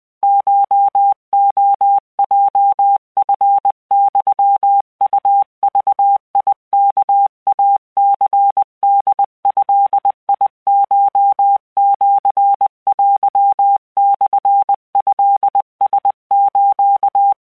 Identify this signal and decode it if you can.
Morse code (Japanese Wabun code), heard as a keyed tone, about 20 wpm, 800 Hz. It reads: コレヲチユウグワイニホドコシテモトラス